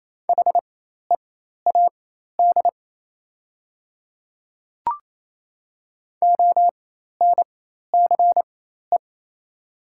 Morse code, heard as a keyed tone, about 28 wpm, spaced as for 13 wpm, 700 Hz. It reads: HEAD ONCE
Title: 4x Character Spacing - Top 300 Words - 28wpm